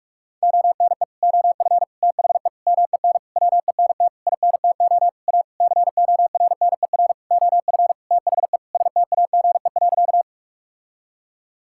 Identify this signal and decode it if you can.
Morse code, heard as a keyed tone, about 41 wpm, 700 Hz. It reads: ONE OF THE MEN WENT INTO A CORNER OF THE STAGE.
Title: Sentences from Top 1600 Words 45wpm